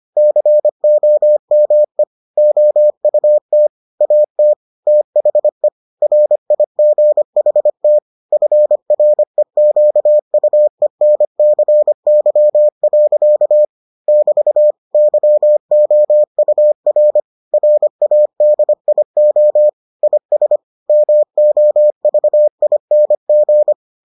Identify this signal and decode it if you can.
Morse code, heard as a keyed tone, about 25 wpm, 600 Hz. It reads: COME OUT AT THE RIGHT FREQUENCY. <BT> YOUR RADIO IS MOVING